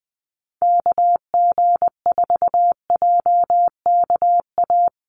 Morse code, heard as a keyed tone, about 20 wpm, 700 Hz. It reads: KG4JKA
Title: USA Call Signs Repeated 2x 20wpm